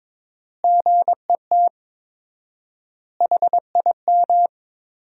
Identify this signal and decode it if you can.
Morse code, heard as a keyed tone, about 22 wpm, 700 Hz. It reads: GET HIM
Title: 4x Word-Spacing - Sentences from Top 400 Words - 22wpm